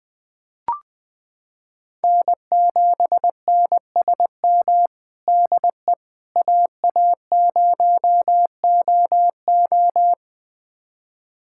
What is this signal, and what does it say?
Morse code, heard as a keyed tone, about 21 wpm, 700 Hz. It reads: N7NSM DE AA0OO
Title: USA Callsign DE Callsign - 5 Characters - 20wpm